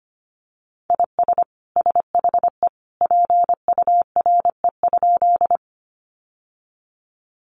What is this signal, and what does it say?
Morse code, heard as a keyed tone, about 25 wpm, 700 Hz. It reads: IS SHE PURE?
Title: Sentences from Top 1500 Words 25wpm